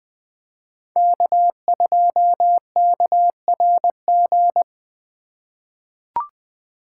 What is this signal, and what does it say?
Morse code, heard as a keyed tone, about 20 wpm, 700 Hz. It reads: K2KRG
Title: USA Call Signs - 5 Characters - 4x Character Spacing - 20wpm